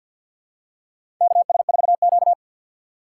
Morse code, heard as a keyed tone, about 48 wpm, 690 Hz. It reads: KS4Q
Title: USA Call Signs - 6x Character Spacing - 50wpm